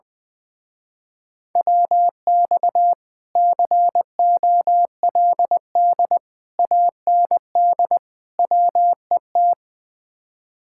Morse code, heard as a keyed tone, about 20 wpm, 700 Hz. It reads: WX COLD AND WET
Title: Speed-Racing Common QSO Phrases Encore - 20wpm